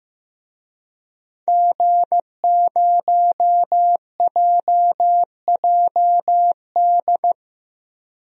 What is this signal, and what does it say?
Morse code, heard as a keyed tone, about 15 wpm, 700 Hz. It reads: G0JJD
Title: DXCC-100 Call Signs - 5 Characters - 15wpm